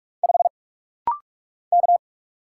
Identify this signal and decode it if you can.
Morse code, heard as a keyed tone, about 44 wpm, 700 Hz. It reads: 5 K
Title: Single Letter-Number - Rapid-Fire 45wpm